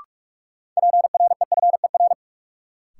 Morse code, heard as a keyed tone, about 46 wpm, 700 Hz. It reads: PREFER
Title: Sentences from Top 1100 Words - New Word Drill - 45wpm